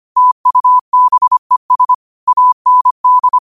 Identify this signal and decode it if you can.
Morse code, heard as a keyed tone, about 25 wpm, 1000 Hz. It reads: TUBES AND
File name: qotdfiles / 25wpm